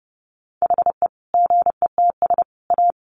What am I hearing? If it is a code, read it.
Morse code, heard as a keyed tone, about 30 wpm, 700 Hz. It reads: HE GETS A
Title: Additional Easy Phrases 30wpm